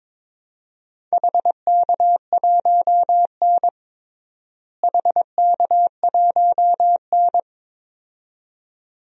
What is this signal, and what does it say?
Morse code, heard as a keyed tone, about 22 wpm, 700 Hz. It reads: HK1N HK1N
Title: DXCC-100 Call Signs - 4 Characters - Repeated 2x - 22wpm